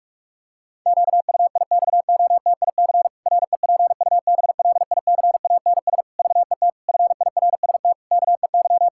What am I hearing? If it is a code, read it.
Morse code, heard as a keyed tone, about 46 wpm, 700 Hz. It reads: QUIXOTIC REPUBLICANS VET FIRST KEY
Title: Pangram Challenge v2 - 45wpm